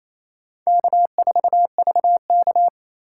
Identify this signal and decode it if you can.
Morse code, heard as a keyed tone, about 28 wpm, 700 Hz. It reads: K4VK
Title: USA Call Signs - 4 Characters - 8x Character Spacing - 28wpm